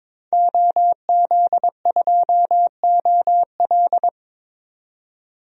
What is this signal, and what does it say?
Morse code, heard as a keyed tone, about 21 wpm, 700 Hz. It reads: OZ2OL
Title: DXCC-100 Call Signs - 5 Characters - 22wpm